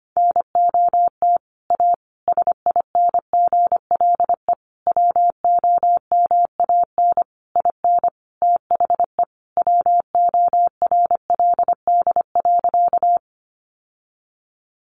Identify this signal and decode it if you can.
Morse code, heard as a keyed tone, about 25 wpm, 700 Hz. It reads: NOT A SINGLE WOMAN IN THE WORLD.
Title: Sentences from Top 1100 Words 25wpm